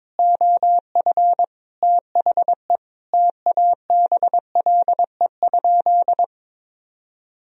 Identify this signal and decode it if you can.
Morse code, heard as a keyed tone, about 22 wpm, 700 Hz. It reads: OF THE TABLE?
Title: Sentences from Top 1000 Words - Review Part 3 - 22wpm